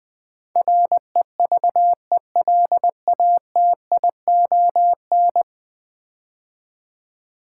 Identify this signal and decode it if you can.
Morse code, heard as a keyed tone, about 20 wpm, 700 Hz. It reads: REVELATION
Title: Build A Word - 5 Letters At A Time - 10-Letter Words - 20wpm